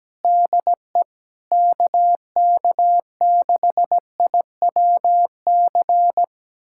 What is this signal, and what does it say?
Morse code, heard as a keyed tone, about 17 wpm, 700 Hz. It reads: DE KK6IWC
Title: USA Callsign DE Callsign - 6 Characters - 17wpm